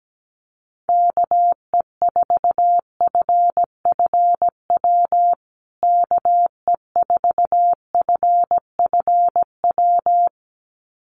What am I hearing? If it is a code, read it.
Morse code, heard as a keyed tone, about 17 wpm, 700 Hz. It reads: KE4FFW KE4FFW
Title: USA Call Signs Repeated 2x 17wpm